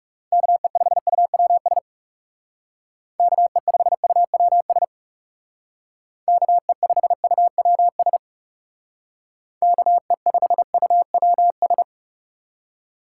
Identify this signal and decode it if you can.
Morse code, heard as a keyed tone, about 39 wpm, 700 Hz. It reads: KE5UWS KE5UWS KE5UWS KE5UWS